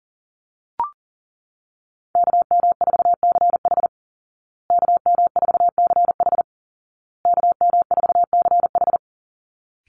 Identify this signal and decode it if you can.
Morse code, heard as a keyed tone, about 40 wpm, 700 Hz. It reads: KM4CH KM4CH KM4CH
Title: Call Signs Repeated 2x - 40wpm